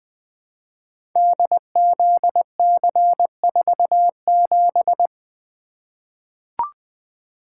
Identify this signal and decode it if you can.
Morse code, heard as a keyed tone, about 21 wpm, 700 Hz. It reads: DZC47
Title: Code Groups - 5 Letter:Numbers - 20wpm